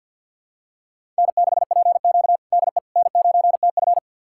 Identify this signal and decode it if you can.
Morse code, heard as a keyed tone, about 50 wpm, 700 Hz. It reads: N6PQ DE N9TF
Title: USA Callsign DE Callsign - 4 Characters - 50wpm